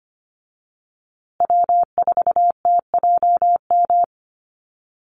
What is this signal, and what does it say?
Morse code, heard as a keyed tone, about 25 wpm, 700 Hz. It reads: W4TJM
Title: USA Call Signs Repeated 2x 25wpm